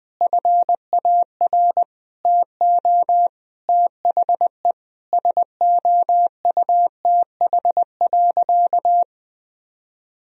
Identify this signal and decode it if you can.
Morse code, heard as a keyed tone, about 21 wpm, 700 Hz. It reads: FAR TO THE SOUTH.
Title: Sentences from Top 1000 Words - Review Part 2 - 20wpm